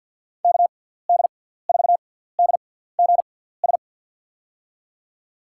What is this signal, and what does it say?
Morse code, heard as a keyed tone, about 46 wpm, spaced as for 17 wpm, 700 Hz. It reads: KD4DGS
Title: USA Call Signs - 6 Characters - 6x Character Spacing - 50wpm